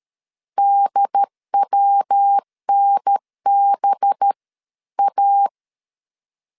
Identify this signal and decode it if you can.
Morse code, heard as a keyed tone, about 13 wpm, 800 Hz. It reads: DWNB A
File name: CW / DET / 1 9 1500C S A 12wpm 3800